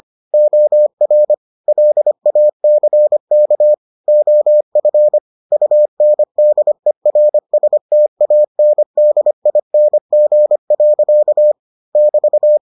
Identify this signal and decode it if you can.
Morse code, heard as a keyed tone, about 25 wpm, 600 Hz. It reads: OR LACK OF UNDERSTANDING. <BT>